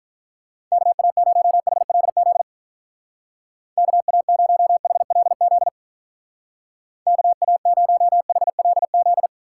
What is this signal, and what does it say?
Morse code, heard as a keyed tone, about 44 wpm, 700 Hz. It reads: KA0HLZ KA0HLZ KA0HLZ